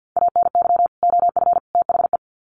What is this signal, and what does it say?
Morse code, heard as a keyed tone, about 50 wpm, 700 Hz. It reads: ANY OF THE
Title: Sets of 3 Words 50wpm